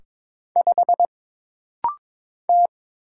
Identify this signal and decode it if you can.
Morse code, heard as a keyed tone, about 22 wpm, 700 Hz. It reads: 5 T
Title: Single Letter-Number Alt - Rapid-Fire 22wpm